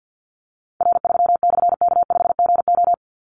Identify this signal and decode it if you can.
Morse code, heard as a keyed tone, about 50 wpm, 700 Hz. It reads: R3/K5GO
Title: Call Signs Repeated 3x - 50wpm